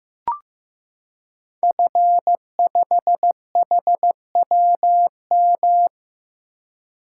Morse code, heard as a keyed tone, about 15 wpm, 700 Hz. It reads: F5HWM
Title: Code Groups - 5 Letter:Numbers - 15wpm